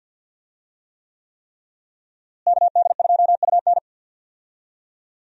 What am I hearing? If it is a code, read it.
Morse code, heard as a keyed tone, about 51 wpm, 700 Hz. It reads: KD2UN
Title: USA Call Signs - 5 Characters - Repeated 3x - 50wpm